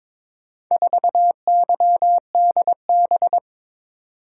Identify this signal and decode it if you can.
Morse code, heard as a keyed tone, about 22 wpm, 700 Hz. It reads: 4YDB
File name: practice / 22wpm / Four Letter-Number Combinations - 22wpm